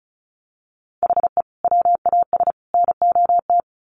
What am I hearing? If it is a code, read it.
Morse code, heard as a keyed tone, about 34 wpm, 700 Hz. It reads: HE WAS NOT
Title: Sets of 3 Words 35wpm